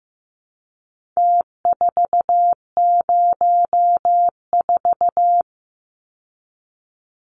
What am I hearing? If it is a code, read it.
Morse code, heard as a keyed tone, about 15 wpm, 700 Hz. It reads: T404
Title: Numbers - Contest Style - 4-Digit 15wpm